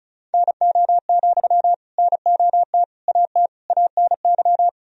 Morse code, heard as a keyed tone, about 36 wpm, 700 Hz. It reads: NO, NOT AT ANY
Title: Sentences from Top 1000 Words - Review Part 4 - 35wpm